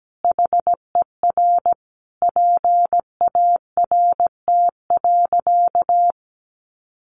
Morse code, heard as a keyed tone, about 17 wpm, 700 Hz. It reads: HER PART.
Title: Sentences from Top 1200 Words 17wpm